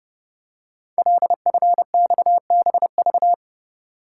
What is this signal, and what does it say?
Morse code, heard as a keyed tone, about 30 wpm, 700 Hz. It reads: LFXBV
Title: Code Groups - 5 Letters - 30wpm